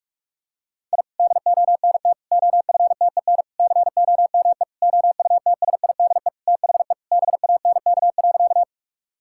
Morse code, heard as a keyed tone, about 44 wpm, 700 Hz. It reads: I DONT OFTEN COME OUTSIDE THE BANK.
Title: Sentences from Top 1000 Words - Review Part 6 - 45wpm